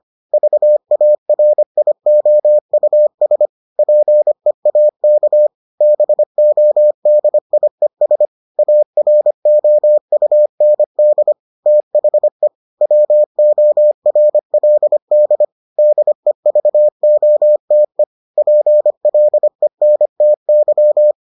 Morse code, heard as a keyed tone, about 25 wpm, 600 Hz. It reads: VARIOUS PEAK BODIES AROUND THE WORLD DEVOTE PLENTY